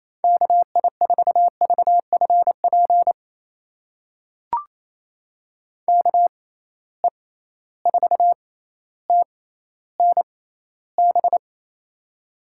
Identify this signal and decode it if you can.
Morse code, heard as a keyed tone, about 28 wpm, spaced as for 14 wpm, 700 Hz. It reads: KI4VFP KE4TNB
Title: USA Call Signs - 6 Characters - 6x Character Spacing - 28wpm